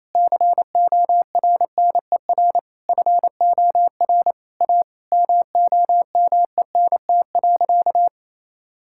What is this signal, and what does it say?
Morse code, heard as a keyed tone, about 28 wpm, 700 Hz. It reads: CORNER FOR A MOMENT.
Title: Sentences from Top 1600 Words 28wpm